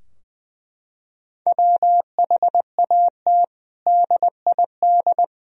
Morse code, heard as a keyed tone, about 20 wpm, 700 Hz. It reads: WHAT DID
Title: Sentences from Top 1000 Words - Review Part 3 - 20wpm